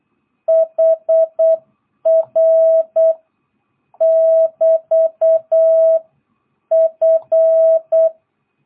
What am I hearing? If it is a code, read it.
Morse code, heard as a keyed tone, about 8 wpm, 660 Hz. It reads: HR <BT> F
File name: jh8cbh / 2022cw